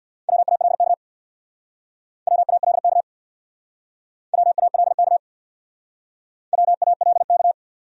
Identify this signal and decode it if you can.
Morse code, heard as a keyed tone, about 68 wpm, 700 Hz. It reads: WALK WALK WALK WALK